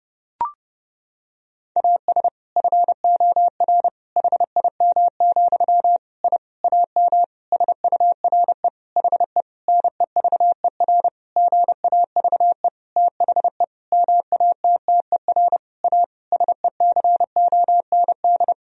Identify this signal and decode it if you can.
Morse code, heard as a keyed tone, about 30 wpm, 700 Hz. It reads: AS FOR HIM, I AM SURE HE NEVER GAVE THE MATTER A SECOND
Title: Sentences from Top 1000 Words - Review Part 2 - 30wpm